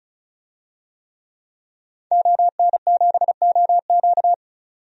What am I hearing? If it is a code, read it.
Morse code, heard as a keyed tone, about 36 wpm, 700 Hz. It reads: ON7OQ
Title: DXCC-100 Call Signs - 5 Characters - 35wpm